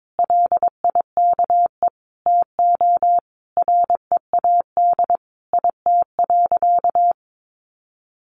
Morse code, heard as a keyed tone, about 22 wpm, 700 Hz. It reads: LIKE TO READ IT.
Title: Sentences from Top 500 Words 22wpm